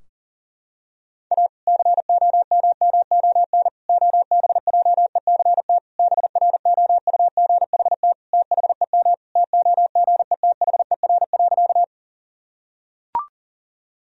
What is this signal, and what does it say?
Morse code, heard as a keyed tone, about 40 wpm, 700 Hz. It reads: A COMMON OBJECT BROUGHT THEM TOGETHER.
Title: Sentences from Top 1000 Words - Review Part 3 - 40wpm